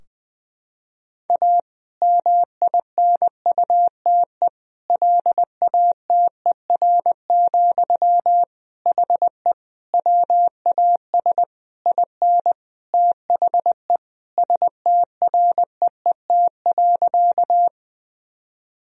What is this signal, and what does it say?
Morse code, heard as a keyed tone, about 20 wpm, 700 Hz. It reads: A MINUTE LATER, HE WAS IN THE STREET.
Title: Sentences from Top 1000 Words - Review Part 4 - 20wpm